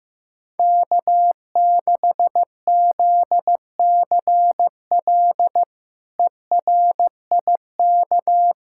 Morse code, heard as a keyed tone, about 15 wpm, 700 Hz. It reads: K6ZCL ERIK
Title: K1USN SST QSO Practice - 15wpm